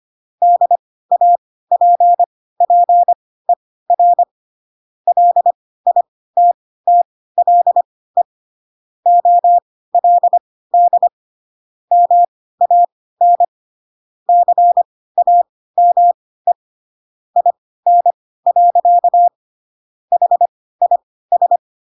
Morse code, heard as a keyed tone, about 25 wpm, spaced as for 16 wpm, 700 Hz. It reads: DAPPER LITTLE OLD MAN CAME IN. HIS
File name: Pinocchio / Pinocchio ch2 10 25 ik1hge